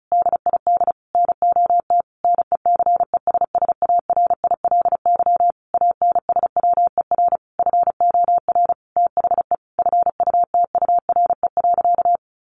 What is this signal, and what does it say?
Morse code, heard as a keyed tone, about 34 wpm, 700 Hz. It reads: DID NOT NECESSARILY ANSWER FOR THE FUTURE.
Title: Sentences from Top 1400 Words 35wpm